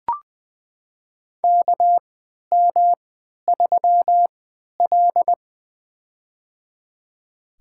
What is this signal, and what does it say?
Morse code, heard as a keyed tone, about 21 wpm, spaced as for 11 wpm, 700 Hz. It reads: KM3L
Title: USA Call Signs - 4 Characters - 3x Character Spacing - 20wpm